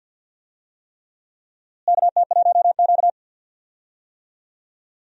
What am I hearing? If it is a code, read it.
Morse code, heard as a keyed tone, about 52 wpm, 700 Hz. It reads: KT1Q
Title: USA Call Signs - 4 Characters - Repeated 3x - 50wpm